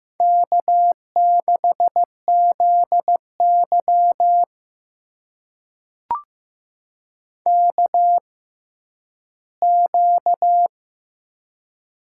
Morse code, heard as a keyed tone, about 15 wpm, spaced as for 10 wpm, 700 Hz. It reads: K6ZY KQ
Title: USA Call Signs - 4 Characters - 6x Character Spacing - 15wpm